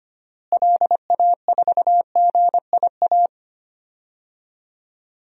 Morse code, heard as a keyed tone, about 25 wpm, 700 Hz. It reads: LA4GIA
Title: DXCC-100 Call Signs - 6 Characters - Repeated 2x - 25wpm